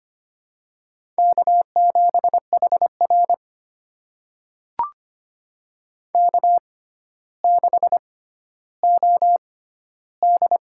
Morse code, heard as a keyed tone, about 25 wpm, spaced as for 13 wpm, 700 Hz. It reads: K7HR K6OD
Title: USA Call Signs - 4 Characters - 6x Character Spacing - 25wpm